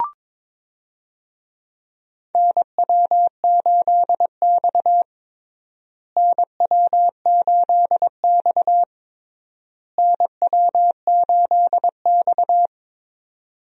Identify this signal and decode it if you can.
Morse code, heard as a keyed tone, about 22 wpm, 700 Hz. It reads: NW8X NW8X NW8X
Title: USA Call Signs - 4 Characters - Repeated 3x - 22wpm